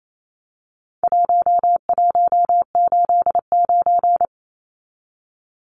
Morse code, heard as a keyed tone, about 28 wpm, 700 Hz. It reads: 1189
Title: Numbers - 4-Digit 28wpm